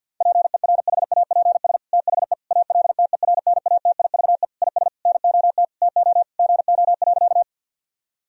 Morse code, heard as a keyed tone, about 50 wpm, 700 Hz. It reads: PERHAPS THE ALTERNATIVE IS NOT TO GO.
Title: Sentences from Top 1000 Words - Review Part 4 - 50wpm